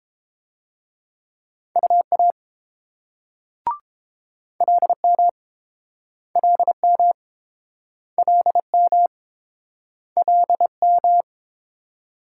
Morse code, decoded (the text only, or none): UA LM LM LM LM